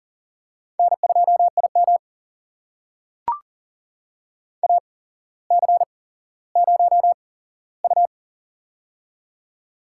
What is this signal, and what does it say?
Morse code, heard as a keyed tone, about 40 wpm, spaced as for 17 wpm, 700 Hz. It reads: N2IM AC0U
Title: USA Call Signs - 8x Character Spacing - 40wpm